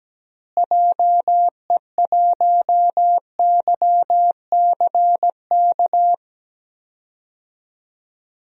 Morse code, heard as a keyed tone, about 17 wpm, 700 Hz. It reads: JE1YCK
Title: DXCC-100 Call Signs - 6 Characters - Repeated 3x - 17wpm